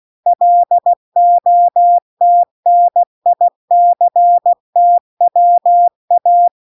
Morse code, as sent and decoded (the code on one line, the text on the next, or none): .-.. --- - -. .. -.-. - .-- .-
LOTNICTWA